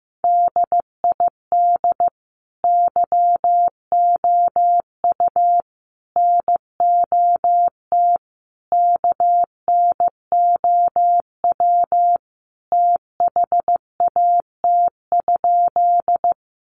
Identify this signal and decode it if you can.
Morse code, heard as a keyed tone, about 15 wpm, 700 Hz. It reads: DID YOU NOT KNOW THAT?
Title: Sentences from Top 600 Words 15wpm